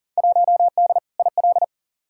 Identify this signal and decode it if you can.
Morse code, heard as a keyed tone, about 40 wpm, 700 Hz. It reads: 1D IL
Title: Field Day Exchanges - 40wpm